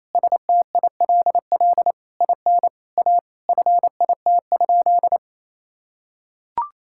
Morse code, heard as a keyed tone, about 28 wpm, 700 Hz. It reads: STILL IN A FIT?
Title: Sentences from Top 1000 Words - Review Part 5 - 28wpm